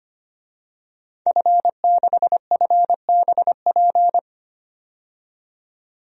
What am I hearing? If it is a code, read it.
Morse code, heard as a keyed tone, about 25 wpm, 700 Hz. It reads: F6FBP
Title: DXCC-100 Call Signs - 5 Characters - 25wpm